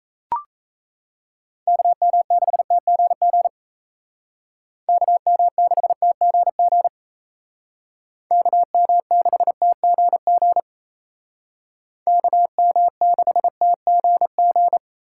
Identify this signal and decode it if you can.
Morse code, heard as a keyed tone, about 36 wpm, 700 Hz. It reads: KM6TGG KM6TGG KM6TGG KM6TGG